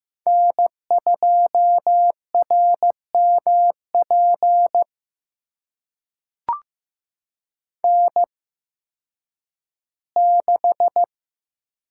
Morse code, heard as a keyed tone, about 15 wpm, spaced as for 10 wpm, 700 Hz. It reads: N2RMP N6
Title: USA Call Signs - 8x Character Spacing - 15wpm